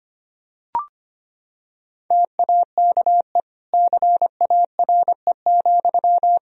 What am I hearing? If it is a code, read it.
Morse code, heard as a keyed tone, about 25 wpm, 700 Hz. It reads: TAKE CARE,
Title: Sentences from Top 1000 Words - Review Part 4 - 25wpm